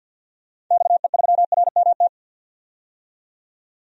Morse code, heard as a keyed tone, about 52 wpm, 700 Hz. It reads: XE3RMT
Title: DXCC-100 Call Signs - 6 Characters - 50wpm